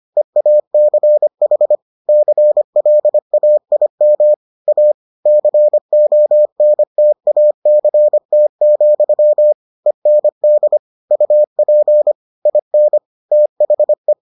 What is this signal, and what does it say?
Morse code, heard as a keyed tone, about 25 wpm, 600 Hz. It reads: EACH CLAIM A CONTACT, END UP IN THE